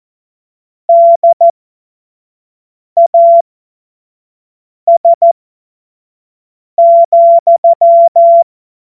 Morse code, heard as a keyed tone, about 14 wpm, spaced as for 5 wpm, 680 Hz. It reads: DAS,
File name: Morsekurs / Witze / Witz 19 März 2024 25 F70